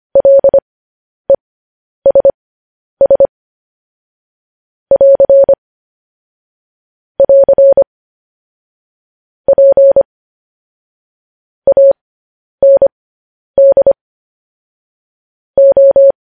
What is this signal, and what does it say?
Morse code, heard as a keyed tone, about 25 wpm, spaced as for 10 wpm, 550 Hz. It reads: LESS <AR> <AR> P AND O